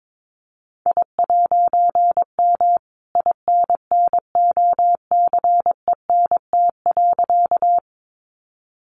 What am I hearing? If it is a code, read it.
Morse code, heard as a keyed tone, about 22 wpm, 700 Hz. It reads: I'M INNOCENT.
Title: A Sound Of Thunder 22wpm